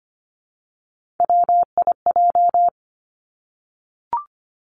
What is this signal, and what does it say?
Morse code, heard as a keyed tone, about 25 wpm, 700 Hz. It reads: WIJ
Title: Three Challenging Letter-Number Combinations 25wpm